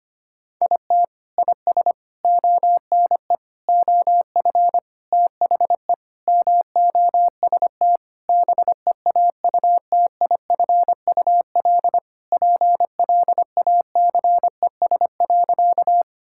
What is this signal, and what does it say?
Morse code, heard as a keyed tone, about 25 wpm, 700 Hz. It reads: IT IS ONE OF THE MOST BEAUTIFUL PLACES.
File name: practice / 25wpm / Sentences with 3-Word Phrases - 25wpm